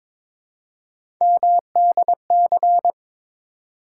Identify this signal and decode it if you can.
Morse code, heard as a keyed tone, about 23 wpm, 700 Hz. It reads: MDC
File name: ARRL Sections v2 - 22wpm